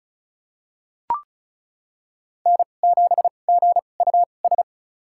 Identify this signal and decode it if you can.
Morse code, heard as a keyed tone, about 36 wpm, spaced as for 25 wpm, 700 Hz. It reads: N7GUS
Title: USA Call Signs - 5 Characters - 2x Character Spacing - 35wpm